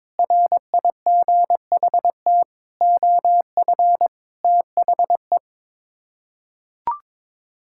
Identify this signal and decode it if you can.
Morse code, heard as a keyed tone, about 22 wpm, 700 Hz. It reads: RIGHT OF THE
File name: Sets of 9 Words - 22wpm